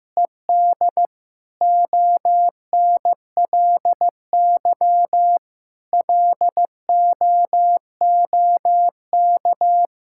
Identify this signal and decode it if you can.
Morse code, heard as a keyed tone, about 16 wpm, 700 Hz. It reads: ED ONLY LOOK
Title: Sentences from Top 1600 Words 15wpm